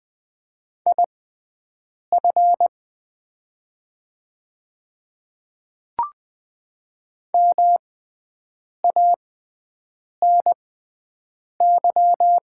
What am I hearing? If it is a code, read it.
Morse code, heard as a keyed tone, about 20 wpm, spaced as for 7 wpm, 700 Hz. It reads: IF MANY
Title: 6x Character Spacing - Top 200 Words - 20wpm